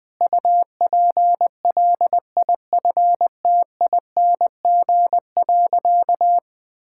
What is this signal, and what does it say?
Morse code, heard as a keyed tone, about 20 wpm, 700 Hz. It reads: UPLIFTING.
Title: All-Age Appropriate Jokes - 20wpm